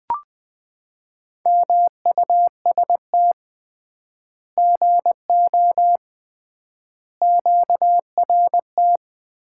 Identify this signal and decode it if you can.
Morse code, heard as a keyed tone, about 21 wpm, 700 Hz. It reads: MUST GO QRT
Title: Common QSO Phrases - 3x Word-Spacing - 20wpm